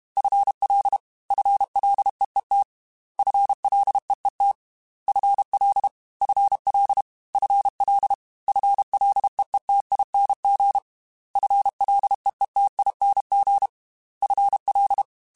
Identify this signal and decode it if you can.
Morse code, heard as a keyed tone, about 32 wpm, 800 Hz. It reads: FL FLEET FLEET FL FL FL FLEETING FLEETING FL
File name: FL Prefixes 3x2 @30